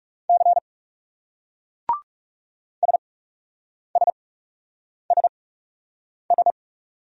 Morse code, heard as a keyed tone, about 37 wpm, spaced as for 8 wpm, 700 Hz. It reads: C SSSS
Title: Speed-Racing Single Letters 30wpm